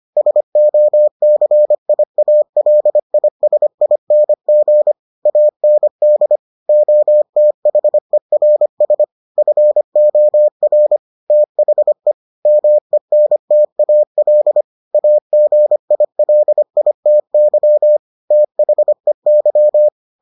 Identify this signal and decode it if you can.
Morse code, heard as a keyed tone, about 25 wpm, 600 Hz. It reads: SOCIALISING AND OTHERS FOR THE MENTAL AGILITY THEY